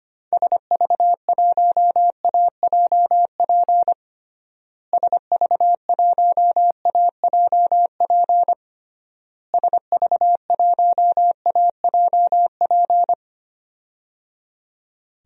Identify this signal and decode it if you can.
Morse code, heard as a keyed tone, about 25 wpm, 700 Hz. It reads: SV1AJP SV1AJP SV1AJP
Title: DXCC-100 Call Signs - 6 Characters - Repeated 3x - 25wpm